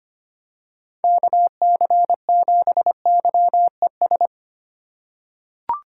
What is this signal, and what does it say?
Morse code, heard as a keyed tone, about 25 wpm, 700 Hz. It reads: KC7YES
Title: USA Call Signs - 2x Character Spacing - 25wpm